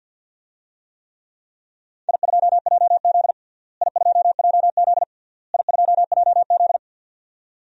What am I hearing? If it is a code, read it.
Morse code, heard as a keyed tone, about 52 wpm, 700 Hz. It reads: I2JZ I2JZ I2JZ
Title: DXCC-100 Call Signs - 4 Characters - Repeated 3x - 50wpm